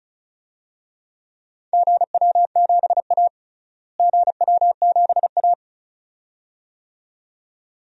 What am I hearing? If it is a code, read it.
Morse code, heard as a keyed tone, about 36 wpm, 700 Hz. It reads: GW7A GW7A
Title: DXCC-100 Call Signs - 4 Characters - Repeated 2x - 35wpm